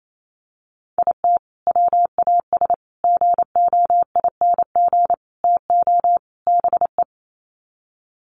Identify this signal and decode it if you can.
Morse code, heard as a keyed tone, about 28 wpm, 700 Hz. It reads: IT WAS GOING TO BE
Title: Sets of 5 Words 28wpm